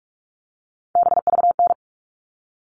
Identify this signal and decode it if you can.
Morse code, heard as a keyed tone, about 44 wpm, 700 Hz. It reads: BVN
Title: All Cut Numbers - 3-Digit 45wpm